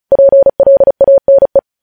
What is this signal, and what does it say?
Morse code, heard as a keyed tone, about 36 wpm, 550 Hz. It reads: PLANE
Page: news headlines in morse code at 35 wpm